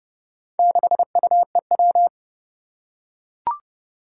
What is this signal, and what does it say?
Morse code, heard as a keyed tone, about 28 wpm, 700 Hz. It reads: 6UEW
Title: Four Challenging Letter-Number Combinations 30wpm